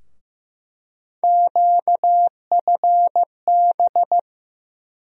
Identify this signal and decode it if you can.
Morse code, heard as a keyed tone, about 15 wpm, 700 Hz. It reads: QFB
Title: USA Call Sign Postfixes - 3 Letters 15wpm